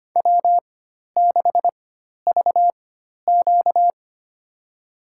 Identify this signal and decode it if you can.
Morse code, heard as a keyed tone, about 25 wpm, spaced as for 12 wpm, 690 Hz. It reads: W6VQ
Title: USA Call Signs - 4 Characters - 4x Character Spacing - 25wpm